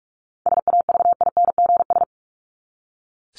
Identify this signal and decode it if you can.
Morse code, heard as a keyed tone, about 44 wpm, 700 Hz. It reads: SAVINGS
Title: Top 1001-1250 Words 45wpm